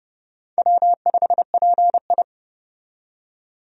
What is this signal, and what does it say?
Morse code, heard as a keyed tone, about 30 wpm, 700 Hz. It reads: W5PI
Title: USA Call Signs - 4 Characters - 30wpm